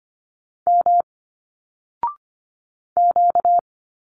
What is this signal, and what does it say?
Morse code, heard as a keyed tone, about 25 wpm, 700 Hz. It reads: M Q
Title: Single Letters 25wpm